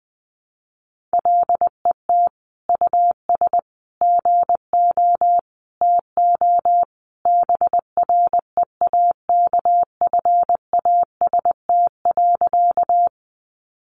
Morse code, heard as a keyed tone, about 20 wpm, 700 Hz. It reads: LET US GO TO BREAKFAST.
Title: Sentences from Top 1500 Words 20wpm